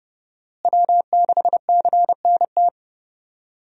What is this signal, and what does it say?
Morse code, heard as a keyed tone, about 30 wpm, 700 Hz. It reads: W6CNT
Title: USA Call Signs - 5 Characters - Repeated 2x - 30wpm